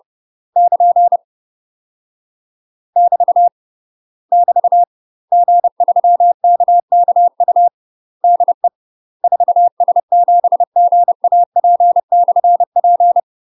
Morse code, heard as a keyed tone, about 30 wpm, 700 Hz. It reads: <KN> <BT> <BT> G3KKU DE 4S7GAP/P
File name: UK MASTER QSOs TRAINING / 30WPM / UK MASTER QSOs 0051